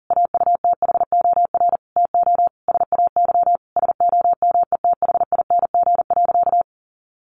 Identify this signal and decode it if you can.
Morse code, heard as a keyed tone, about 40 wpm, 700 Hz. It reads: AUTHOR TO SAY SOMETHING.
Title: Sentences from Top 1200 Words 40wpm